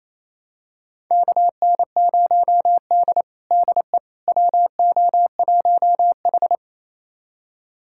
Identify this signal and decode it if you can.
Morse code, heard as a keyed tone, about 28 wpm, 700 Hz. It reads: KN0D DE WO1H
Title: USA Callsign DE Callsign - 4 Characters - 28wpm